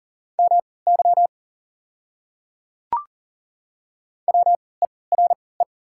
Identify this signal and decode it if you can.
Morse code, heard as a keyed tone, about 40 wpm, spaced as for 23 wpm, 690 Hz. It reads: MY WERE
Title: 3x Character Spacing - Top 100 Words - 40wpm